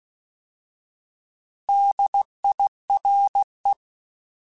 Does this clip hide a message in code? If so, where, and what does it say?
Yes: Morse code from 1.7 to 3.7 s, heard as a keyed tone, about 16 wpm, 800 Hz. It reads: DIRE